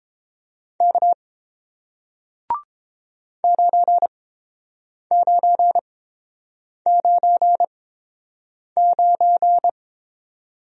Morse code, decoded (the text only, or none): K 9999